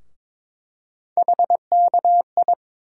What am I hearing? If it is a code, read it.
Morse code, heard as a keyed tone, about 22 wpm, 700 Hz. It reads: HKI
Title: USA Call Sign Postfixes - 3 Letters 22wpm